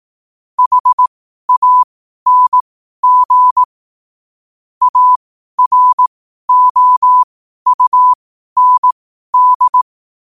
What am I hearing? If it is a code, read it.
Morse code, heard as a keyed tone, about 18 wpm, spaced as for 12 wpm, 1000 Hz. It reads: HANG AROUND
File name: qotdfiles / 12wpm